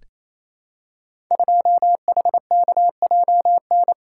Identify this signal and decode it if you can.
Morse code, heard as a keyed tone, about 28 wpm, 690 Hz. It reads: 2HKJN
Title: Code Groups - 5 Letter:Numbers - 28wpm